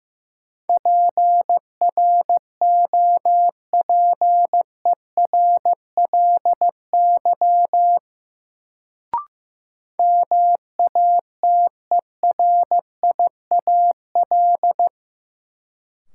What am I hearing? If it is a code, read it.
Morse code, heard as a keyed tone, about 15 wpm, 700 Hz. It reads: PROPERLY MATERIAL
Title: Sentences from Top 1100 Words - New Word Drill Alt - 15wpm